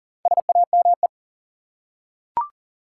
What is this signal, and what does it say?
Morse code, heard as a keyed tone, about 40 wpm, 700 Hz. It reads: SAME
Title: Four Letter Words - 40wpm